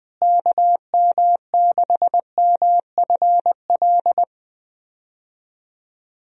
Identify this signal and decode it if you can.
Morse code, heard as a keyed tone, about 21 wpm, 700 Hz. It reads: KM6MFL
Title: USA Call Signs - 6 Characters - Repeated 2x - 20wpm